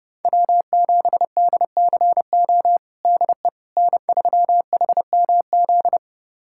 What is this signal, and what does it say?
Morse code, heard as a keyed tone, about 30 wpm, 700 Hz. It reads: W7DCO DE N3HMZ
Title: USA Callsign DE Callsign - 5 Characters - 30wpm